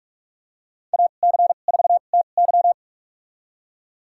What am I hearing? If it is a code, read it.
Morse code, heard as a keyed tone, about 41 wpm, spaced as for 31 wpm, 700 Hz. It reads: AC4TY
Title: USA Call Signs - 5 Characters - 2x Character Spacing - 45wpm